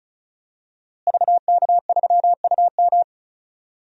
Morse code, heard as a keyed tone, about 34 wpm, 700 Hz. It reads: VK3UM
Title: DXCC-100 Call Signs - 5 Characters - Repeated 3x - 35wpm